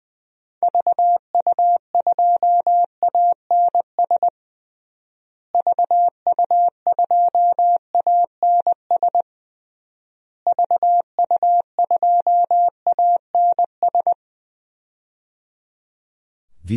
Morse code, heard as a keyed tone, about 20 wpm, 700 Hz. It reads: VU2ANS VU2ANS VU2ANS
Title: DXCC-100 Call Signs - 6 Characters - Repeated 3x - 20wpm